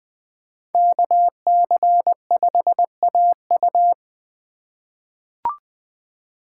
Morse code, heard as a keyed tone, about 20 wpm, 700 Hz. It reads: KC5AU
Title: USA Call Signs - 5 Characters - 2x Character Spacing - 20wpm